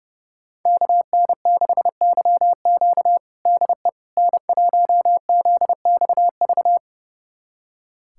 Morse code, heard as a keyed tone, about 30 wpm, 700 Hz. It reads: KN6YQ DE N1ZXV
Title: USA Callsign DE Callsign - 5 Characters - 30wpm